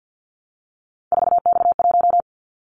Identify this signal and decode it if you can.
Morse code, heard as a keyed tone, about 50 wpm, 700 Hz. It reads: 4X1
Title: Callsign Prefixes for Top 50 DXCC - 50wpm